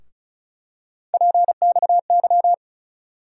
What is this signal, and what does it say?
Morse code, heard as a keyed tone, about 36 wpm, 700 Hz. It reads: PXY
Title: USA Call Sign Postfixes - 3 Letters 35wpm